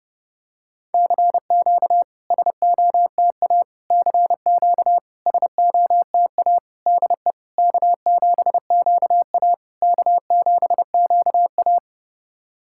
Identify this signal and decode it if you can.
Morse code, heard as a keyed tone, about 30 wpm, 690 Hz. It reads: CQ SOTA CQ SOTA DE K7QA K7QA
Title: SOTA QSO Practice 30wpm